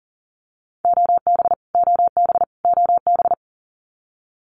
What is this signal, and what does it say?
Morse code, heard as a keyed tone, about 40 wpm, 700 Hz. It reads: OB OB OB
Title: Abbreviations Repeated 3x - 40wpm